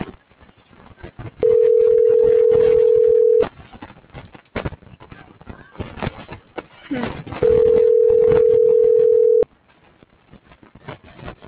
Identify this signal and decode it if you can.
Ringback tone (North America), 440 + 480 Hz.